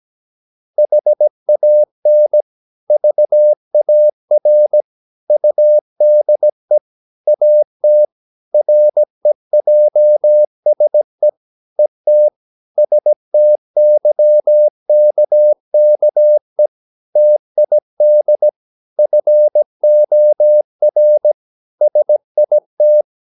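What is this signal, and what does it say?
Morse code, heard as a keyed tone, about 17 wpm, 600 Hz. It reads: HAN VAR UDE AT REJSE ET STYKKE TID FOR SIT
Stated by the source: Slow (QRS) Morse practice at 17 wpm, from CW med Gnister.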